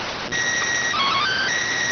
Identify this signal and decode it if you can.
SSTV (slow-scan television), PD 290.